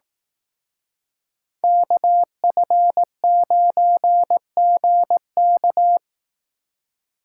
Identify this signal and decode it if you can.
Morse code, heard as a keyed tone, about 18 wpm, 700 Hz. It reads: KF9GK
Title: Speed Racing - USA 2x2 Call Signs 15wpm